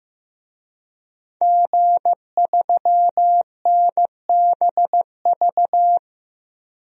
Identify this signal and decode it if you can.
Morse code, heard as a keyed tone, about 15 wpm, 700 Hz. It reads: G3NBV
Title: DXCC-100 Call Signs - 5 Characters - 15wpm